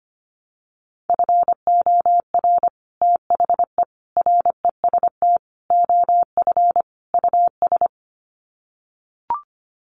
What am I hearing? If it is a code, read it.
Morse code, heard as a keyed tone, about 25 wpm, 700 Hz. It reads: FOR THE REST OF US
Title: Sets of 5 Words Encore 25wpm